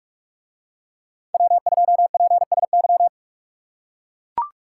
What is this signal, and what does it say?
Morse code, heard as a keyed tone, about 41 wpm, 700 Hz. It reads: W2PSY